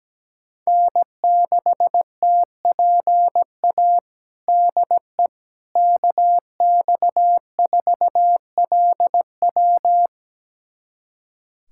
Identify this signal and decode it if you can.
Morse code, heard as a keyed tone, about 17 wpm, 700 Hz. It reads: N6TPA DE KX4LW
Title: USA Callsign DE Callsign - 5 Characters - 17wpm